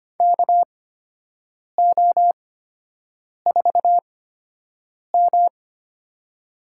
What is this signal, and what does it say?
Morse code, heard as a keyed tone, about 25 wpm, spaced as for 7 wpm, 700 Hz. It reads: KO4M
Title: USA Call Signs - 4 Characters - 8x Character Spacing - 25wpm